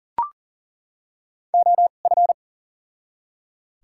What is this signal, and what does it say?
Morse code, heard as a keyed tone, about 40 wpm, spaced as for 29 wpm, 700 Hz. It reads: OF
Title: 2x Character Spacing - Top 300 Words - 40wpm